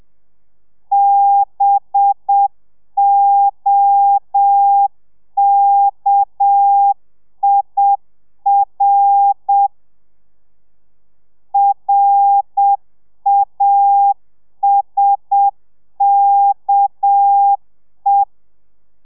Morse code, heard as a keyed tone, about 7 wpm, 800 Hz. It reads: BOKIR RASKE